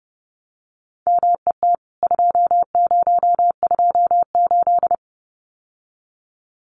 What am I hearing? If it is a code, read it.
Morse code, heard as a keyed tone, about 30 wpm, 700 Hz. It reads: MET 2028
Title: CWT Exchange 30wpm